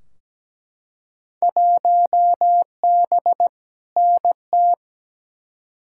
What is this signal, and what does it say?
Morse code, heard as a keyed tone, about 17 wpm, 700 Hz. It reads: 1B NT
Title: Field Day Exchanges Encore - 17wpm